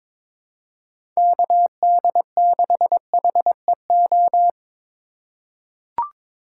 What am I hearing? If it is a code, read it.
Morse code, heard as a keyed tone, about 22 wpm, 700 Hz. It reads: KD6HEO
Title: USA Call Signs - 3x Character Spacing - 22wpm